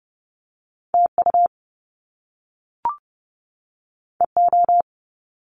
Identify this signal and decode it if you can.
Morse code, heard as a keyed tone, about 30 wpm, 700 Hz. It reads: TU EO
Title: Top Two Letter Combinations in Words 30wpm